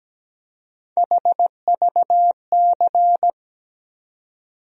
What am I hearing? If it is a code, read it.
Morse code, heard as a keyed tone, about 17 wpm, 700 Hz. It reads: HVC
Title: USA Call Sign Postfixes - 3 Letters 17wpm